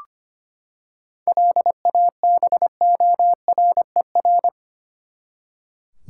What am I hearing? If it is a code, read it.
Morse code, heard as a keyed tone, about 25 wpm, 700 Hz. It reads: LABORER
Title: Practice QSO - Exchange 3 - More Info - 25wpm